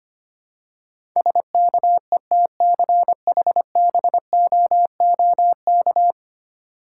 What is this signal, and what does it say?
Morse code, heard as a keyed tone, about 25 wpm, 700 Hz. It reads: SKETCHBOOK
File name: practice / 25wpm / Difficult Words with Hints - 25wpm